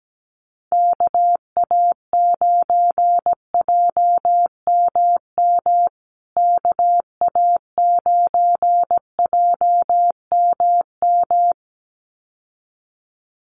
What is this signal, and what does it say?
Morse code, heard as a keyed tone, about 17 wpm, 700 Hz. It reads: KA9JMM KA9JMM
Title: USA Call Signs Repeated 2x 17wpm